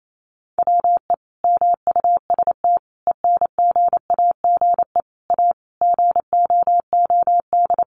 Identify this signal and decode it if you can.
Morse code, heard as a keyed tone, about 28 wpm, 700 Hz. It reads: WE MUST ENGAGE A GOOD
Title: Sentences from Top 1500 Words 28wpm